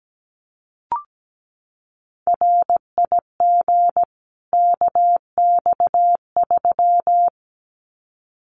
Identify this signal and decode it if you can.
Morse code, heard as a keyed tone, about 17 wpm, 700 Hz. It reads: RIG KX3
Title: Common QSO Phrases 17wpm